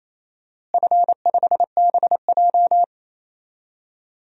Morse code, heard as a keyed tone, about 28 wpm, 700 Hz. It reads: F5BJ
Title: DXCC-100 Call Signs - 4 Characters - Repeated 2x - 28wpm